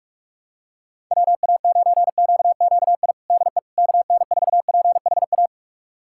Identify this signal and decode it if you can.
Morse code, heard as a keyed tone, about 46 wpm, 690 Hz. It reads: WA9QQI DE KN4PHA
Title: USA Callsign DE Callsign 45wpm